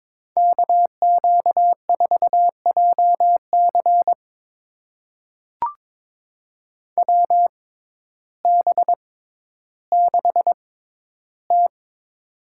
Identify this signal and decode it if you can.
Morse code, heard as a keyed tone, about 22 wpm, spaced as for 12 wpm, 700 Hz. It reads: KQ4JC WB6T
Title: USA Call Signs - 6x Character Spacing - 22wpm